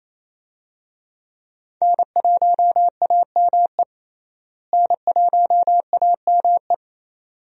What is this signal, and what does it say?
Morse code, heard as a keyed tone, about 28 wpm, 700 Hz. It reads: N1AME N1AME
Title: USA Call Signs - 5 Characters - Repeated 2x - 28wpm